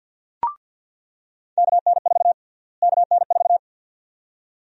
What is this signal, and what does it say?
Morse code, heard as a keyed tone, about 47 wpm, 700 Hz. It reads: KN4 KN4
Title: USA Call Sign Prefixes - Repeated 2x - 50wpm